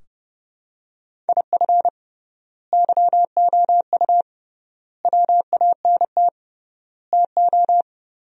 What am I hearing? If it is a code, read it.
Morse code, heard as a keyed tone, about 32 wpm, 700 Hz. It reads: IF YOU WANT TO
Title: 3x Word-Spacing - Sentences from Top 200 Words - 30wpm